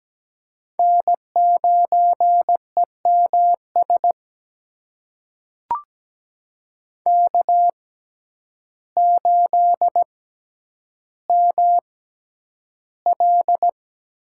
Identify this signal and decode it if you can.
Morse code, heard as a keyed tone, about 17 wpm, spaced as for 9 wpm, 700 Hz. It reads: N9EMS K8ML
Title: USA Call Signs - 5 Characters - 6x Character Spacing - 17wpm